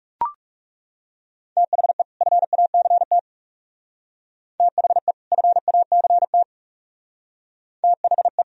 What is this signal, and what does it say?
Morse code, heard as a keyed tone, about 37 wpm, 700 Hz. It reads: THE FACT THE FACT THE
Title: Speed-Racing - Sets of 2 Words 30wpm